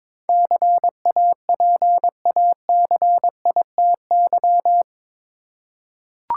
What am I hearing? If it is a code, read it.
Morse code, heard as a keyed tone, about 22 wpm, 700 Hz. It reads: CAPACITY
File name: practice / 22wpm / Eight Letter Words - 22wpm